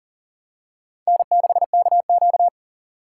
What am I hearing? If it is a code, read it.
Morse code, heard as a keyed tone, about 40 wpm, 700 Hz. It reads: N6KQ
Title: USA Call Signs - 2x Character Spacing - 40wpm